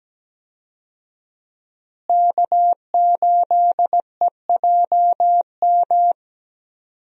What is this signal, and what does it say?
Morse code, heard as a keyed tone, about 17 wpm, 700 Hz. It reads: K8EJM
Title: USA Call Signs - 5 Characters - Repeated 2x - 17wpm